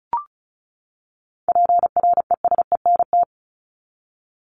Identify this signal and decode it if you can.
Morse code, heard as a keyed tone, about 34 wpm, 700 Hz. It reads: PRESENT
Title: Top 501-750 Words 35wpm